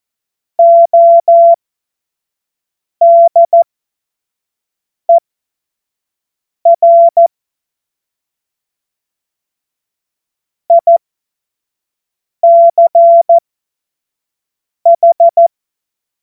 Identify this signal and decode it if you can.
Morse code, heard as a keyed tone, about 14 wpm, spaced as for 5 wpm, 680 Hz. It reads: ODER ICH
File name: Morsekurs / Witze / Witz 10. Januar 2023 25 F70